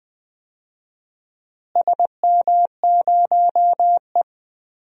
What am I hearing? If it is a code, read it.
Morse code, heard as a keyed tone, about 20 wpm, 700 Hz. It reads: SM0E
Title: DXCC-100 Call Signs - 4 Characters - Repeated 3x - 20wpm